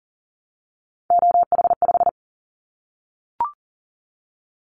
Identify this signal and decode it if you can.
Morse code, heard as a keyed tone, about 40 wpm, 700 Hz. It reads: OH5
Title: Callsign Prefixes for Top 25 DXCC - Repeated 3x - 40wpm